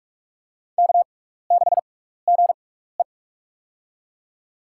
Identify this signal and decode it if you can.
Morse code, heard as a keyed tone, about 47 wpm, spaced as for 16 wpm, 700 Hz. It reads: K6GE
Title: USA Call Signs - 4 Characters - 6x Character Spacing - 45wpm